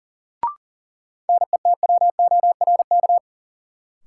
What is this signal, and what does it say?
Morse code, heard as a keyed tone, about 40 wpm, 700 Hz. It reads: NETWORK
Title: Sentences from Top 1000 Words - Review Part 7 - New Word Drill - 40wpm